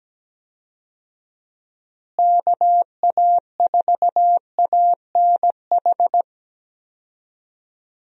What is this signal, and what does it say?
Morse code, heard as a keyed tone, about 17 wpm, 700 Hz. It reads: KA4ANH
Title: USA Call Signs - 6 Characters - Repeated 2x - 17wpm